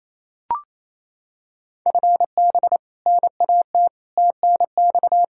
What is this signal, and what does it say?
Morse code, heard as a keyed tone, about 28 wpm, 700 Hz. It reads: FB NAT TNX
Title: Practice QSO - Exchange 3 - More Info - 28wpm